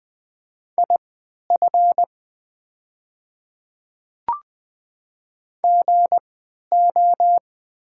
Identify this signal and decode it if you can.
Morse code, heard as a keyed tone, about 20 wpm, spaced as for 11 wpm, 700 Hz. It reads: IF GO
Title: 3x Character Spacing - Top 100 Words - 20wpm